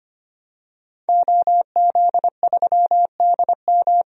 Morse code, heard as a keyed tone, about 25 wpm, 700 Hz. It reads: OZ3DM
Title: DXCC-100 Call Signs - 5 Characters - 25wpm